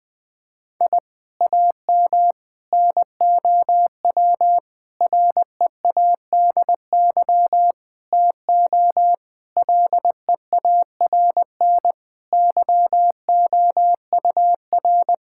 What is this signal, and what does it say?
Morse code, heard as a keyed tone, about 20 wpm, 700 Hz. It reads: I AM NOW READY TO LEARN YOUR
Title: Sentences from Top 1000 Words - Review Part 6 - 20wpm